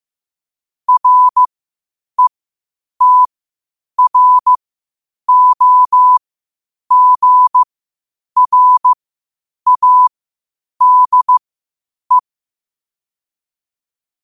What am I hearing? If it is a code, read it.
Morse code, heard as a keyed tone, about 15 wpm, spaced as for 8 wpm, 1000 Hz. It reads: RETROGRADE
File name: qotdfiles / 8wpm